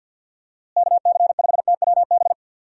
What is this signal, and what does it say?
Morse code, heard as a keyed tone, about 50 wpm, 700 Hz. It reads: KC5TWB
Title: USA Call Signs - 6 Characters - Repeated 2x - 50wpm